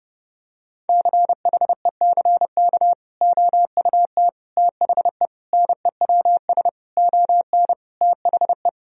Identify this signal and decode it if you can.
Morse code, heard as a keyed tone, about 32 wpm, 700 Hz. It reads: CHECK OUT THE NEWS ON THE
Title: Sentences from Top 1000 Words - Review Part 4 - 30wpm